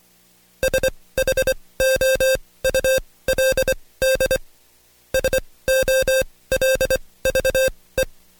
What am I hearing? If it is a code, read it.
Morse code, heard as a keyed tone, about 23 wpm, spaced as for 17 wpm, 530 Hz. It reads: SHOULD SOLVE
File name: Code Quick / AudioDownld2 / Seg135-Stepper8 17WPM